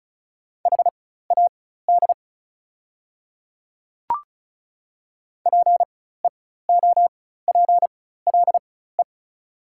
Morse code, heard as a keyed tone, about 34 wpm, spaced as for 16 wpm, 700 Hz. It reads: HAD PEOPLE
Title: 4x Character Spacing - Top 100 Words - 35wpm